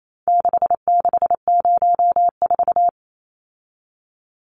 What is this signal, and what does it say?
Morse code, heard as a keyed tone, about 28 wpm, 700 Hz. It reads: 6604
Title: Numbers - 4-Digit 28wpm